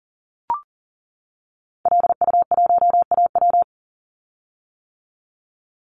Morse code, heard as a keyed tone, about 38 wpm, 700 Hz. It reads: LU1AW
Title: Call Signs - 40wpm